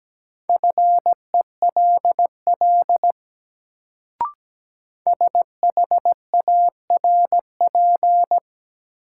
Morse code, heard as a keyed tone, about 17 wpm, 700 Hz. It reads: FELL SHARP
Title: Sentences from Top 1600 Words - New Word Drill Alt - 17wpm